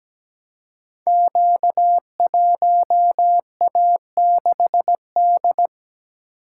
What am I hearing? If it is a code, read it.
Morse code, heard as a keyed tone, about 17 wpm, 700 Hz. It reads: Q1A6D
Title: Code Groups - 5 Letter:Numbers - 17wpm